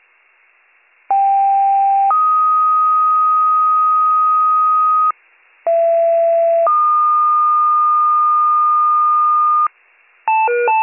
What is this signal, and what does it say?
Two-tone page, heard as a one-second tone, then a three-second tone, about 790 then 1250 Hz, about 670 then 1220 Hz. It is heard from 1.1 to 9.7 s.